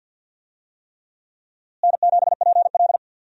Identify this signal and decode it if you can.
Morse code, heard as a keyed tone, about 51 wpm, 700 Hz. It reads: N7PL
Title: USA Call Signs - 4 Characters - Repeated 3x - 50wpm